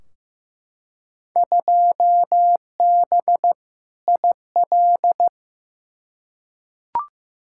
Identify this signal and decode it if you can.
Morse code, heard as a keyed tone, about 15 wpm, 700 Hz. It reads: 2B IL
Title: Field Day Exchanges Encore - 15wpm